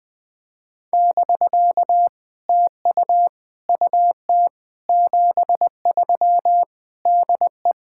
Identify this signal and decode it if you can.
Morse code, heard as a keyed tone, about 20 wpm, 700 Hz. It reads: <BK> TU UT 73 DE
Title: POTA QSO Practice - 20wpm